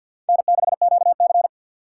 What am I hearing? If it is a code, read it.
Morse code, heard as a keyed tone, about 65 wpm, 700 Hz. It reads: N6QC